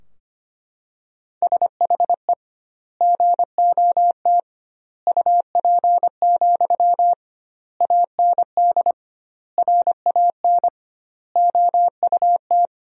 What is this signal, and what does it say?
Morse code, heard as a keyed tone, about 25 wpm, 700 Hz. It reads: SHE GOT UP, AND RAN OUT
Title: 2x Word-Spacing - Sentences from Top 500 Words - 25wpm